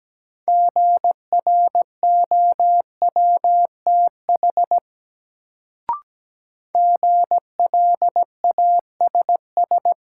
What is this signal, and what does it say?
Morse code, heard as a keyed tone, about 17 wpm, 700 Hz. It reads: GROWTH GLASS
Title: Sentences from Top 1300 Words - New Word Drill Alt - 17wpm